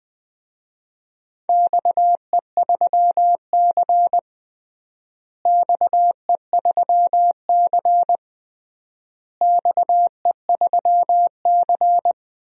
Morse code, heard as a keyed tone, about 20 wpm, 700 Hz. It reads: XE3C XE3C XE3C
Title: DXCC-100 Call Signs - 4 Characters - Repeated 3x - 20wpm